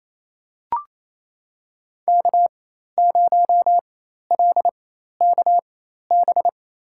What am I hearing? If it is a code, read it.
Morse code, heard as a keyed tone, about 28 wpm, spaced as for 13 wpm, 700 Hz. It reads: K0LKB
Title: USA Call Signs - 5 Characters - 4x Character Spacing - 28wpm